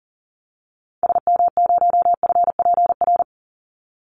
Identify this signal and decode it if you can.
Morse code, heard as a keyed tone, about 40 wpm, 700 Hz. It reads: SM0FPR
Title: Call Signs Repeated 2x - 40wpm